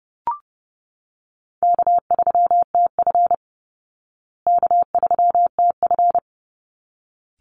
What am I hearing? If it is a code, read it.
Morse code, heard as a keyed tone, about 30 wpm, 700 Hz. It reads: K3TF K3TF
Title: Easy Call Signs Repeated 2x - 30wpm